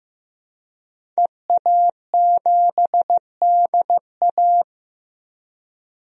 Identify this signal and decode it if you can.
Morse code, heard as a keyed tone, about 15 wpm, 700 Hz. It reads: EA7DA
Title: DXCC-100 Call Signs - 5 Characters - Repeated 3x - 15wpm